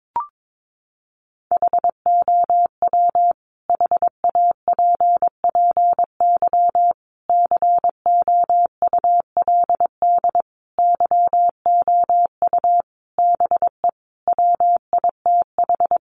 Morse code, heard as a keyed tone, about 22 wpm, 700 Hz. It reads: HOW HAPPY COULD YOU BE WITH
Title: Sentences from Top 1000 Words 22wpm